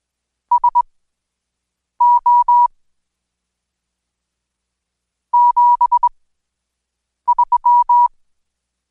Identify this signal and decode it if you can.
Morse code, heard as a keyed tone, about 21 wpm, spaced as for 6 wpm, 970 Hz. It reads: SO 73
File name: Code Quick/AudioDownld2/Seg109-Day14-DayDrill 14